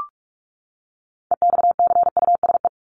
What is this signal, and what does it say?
Morse code, heard as a keyed tone, about 44 wpm, 700 Hz. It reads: EXCUSE
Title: Top 501-750 Words 45wpm